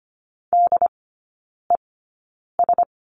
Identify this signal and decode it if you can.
Morse code, heard as a keyed tone, about 25 wpm, spaced as for 9 wpm, 700 Hz. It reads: DES
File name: Lesson 14e - Number 2 - Three Characters 25wpm